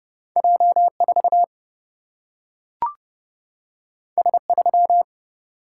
Morse code, heard as a keyed tone, about 30 wpm, 700 Hz. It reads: J 4 S 3
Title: Single-Letter Single-Number 30wpm